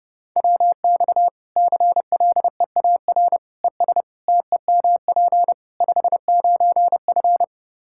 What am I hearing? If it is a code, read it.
Morse code, heard as a keyed tone, about 32 wpm, 700 Hz. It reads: WX CLEAR ES TEMP 59F
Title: K1USN SST QSO Practice 30wpm